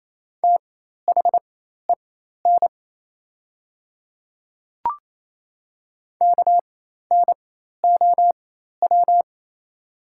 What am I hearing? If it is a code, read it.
Morse code, heard as a keyed tone, about 27 wpm, spaced as for 13 wpm, 690 Hz. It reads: THEN KNOW
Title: 4x Character Spacing - Top 200 Words - 28wpm